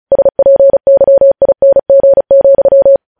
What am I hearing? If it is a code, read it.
Morse code, heard as a keyed tone, about 36 wpm, 550 Hz. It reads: SPYING,
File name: podcast / News headlines-1x-35x35